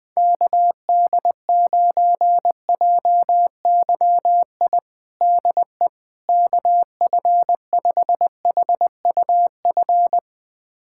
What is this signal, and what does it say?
Morse code, heard as a keyed tone, about 20 wpm, 700 Hz. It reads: KD9JYI DE KF5HUF
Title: USA Callsign DE Callsign 20wpm